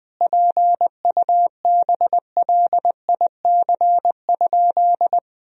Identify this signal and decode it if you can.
Morse code, heard as a keyed tone, about 20 wpm, 700 Hz. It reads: PUBLIC?
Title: Sentences from Top 1000 Words - Review Part 3 - 20wpm